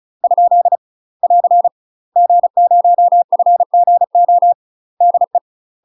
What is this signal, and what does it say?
Morse code, heard as a keyed tone, about 36 wpm, 700 Hz. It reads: ? <AR> G0FGO DE
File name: UK MASTER QSOs TRAINING / 35WPM / UK MASTER QSOs 0035